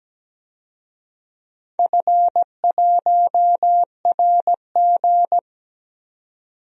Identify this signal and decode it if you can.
Morse code, heard as a keyed tone, about 17 wpm, 700 Hz. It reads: F1RG
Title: DXCC-100 Call Signs - 4 Characters - 17wpm